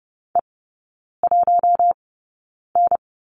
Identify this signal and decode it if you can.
Morse code, heard as a keyed tone, about 30 wpm, spaced as for 9 wpm, 700 Hz. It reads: E1N
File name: course / 30:10wpm / 087 Lesson 15e - Number 5 - Three Characters 30wpm